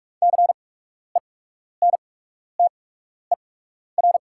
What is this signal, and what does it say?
Morse code, heard as a keyed tone, about 44 wpm, spaced as for 12 wpm, 690 Hz. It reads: CENTER
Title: 8x Character Spacing - Top 500 Words - 45wpm